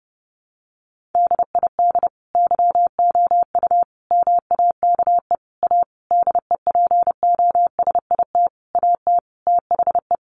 Morse code, heard as a keyed tone, about 30 wpm, 700 Hz. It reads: DID YOU MAKE A DEPOSIT AT THE
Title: Sentences from Top 1300 Words 30wpm